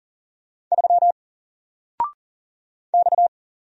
Morse code, heard as a keyed tone, about 40 wpm, 700 Hz. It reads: 3 X
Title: Single Letter-Number 40wpm